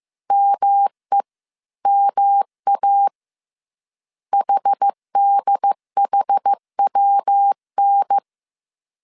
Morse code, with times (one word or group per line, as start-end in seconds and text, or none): 0.3-1.2 ME
1.9-3.1 MA
4.3-8.2 HDHWN